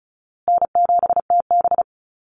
Morse code, heard as a keyed tone, about 34 wpm, 700 Hz. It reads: N7TB
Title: Easy Call Signs Repeated 3x - 35wpm